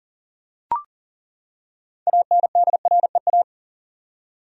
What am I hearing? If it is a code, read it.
Morse code, heard as a keyed tone, about 40 wpm, 700 Hz. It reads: ANDREA
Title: USA Common Names - 40wpm